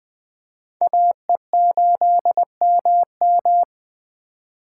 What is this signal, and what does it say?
Morse code, heard as a keyed tone, about 25 wpm, 700 Hz. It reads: AE8MM
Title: Speed Racing - USA 2x2 Call Signs 20wpm